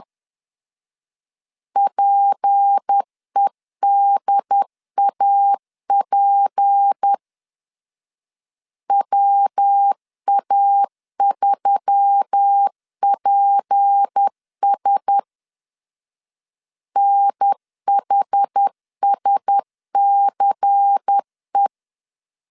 Morse code, heard as a keyed tone, about 10 wpm, 800 Hz. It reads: PEDAP WA3PS NHSCE
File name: LEZIONI / 6 PKX0 / 6 P 10wpm 2432